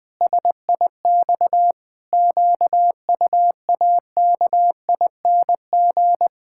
Morse code, heard as a keyed tone, about 20 wpm, 700 Hz. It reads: SIX QUAKING
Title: Pangram Challenge v2 - 20wpm